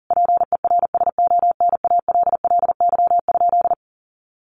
Morse code, heard as a keyed tone, about 40 wpm, 700 Hz. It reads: PERSONALLY?
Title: Sentences from Top 1400 Words 40wpm